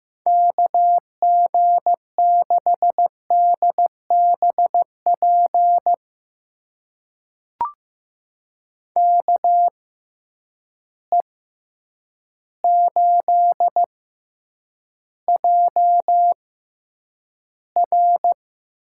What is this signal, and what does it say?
Morse code, heard as a keyed tone, about 15 wpm, spaced as for 8 wpm, 700 Hz. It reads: KG6DBP KE8JR
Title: USA Call Signs - 6x Character Spacing - 15wpm